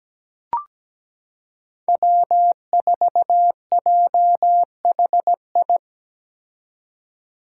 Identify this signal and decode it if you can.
Morse code, heard as a keyed tone, about 17 wpm, 700 Hz. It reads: W4JHI
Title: Five Challenging Letter-Number Combinations 17wpm